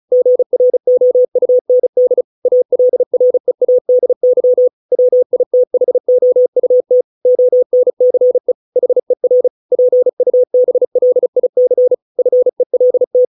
Morse code, heard as a keyed tone, about 34 wpm, 500 Hz. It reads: GROUND ALREADY WITHOUT ONCE HER PUBLIC FELT